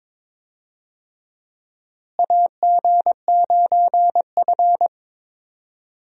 Morse code, heard as a keyed tone, about 21 wpm, 700 Hz. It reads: AG9F
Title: USA Call Signs - 4 Characters - Repeated 3x - 22wpm